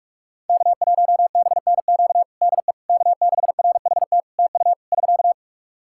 Morse code, heard as a keyed tone, about 46 wpm, 700 Hz. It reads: K1BNQ DE K6RHT TU <SK>
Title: K1USN SST QSO Practice 45wpm